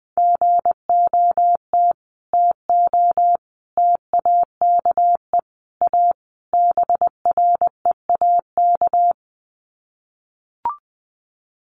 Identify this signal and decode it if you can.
Morse code, heard as a keyed tone, about 20 wpm, 700 Hz. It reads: GOT TO TAKE A BREAK
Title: Sets of 5 Words Encore 20wpm